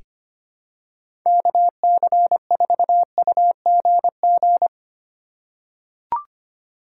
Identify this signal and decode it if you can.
Morse code, heard as a keyed tone, about 25 wpm, 700 Hz. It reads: KC4UGG
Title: USA Call Signs - 6 Characters - 3x Character Spacing - 25wpm